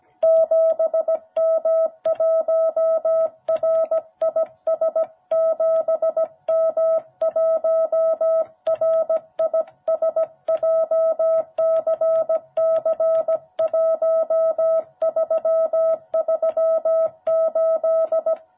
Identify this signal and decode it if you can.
Morse code, heard as a keyed tone, about 17 wpm, 660 Hz. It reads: 7M1RIS 7M1RIS JCC1338
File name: jh8cbh / 2021cw